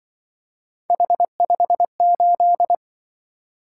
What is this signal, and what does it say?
Morse code, heard as a keyed tone, about 25 wpm, 700 Hz. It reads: H58
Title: Speed-Racing - Three Letter-Number Combinations - 20wpm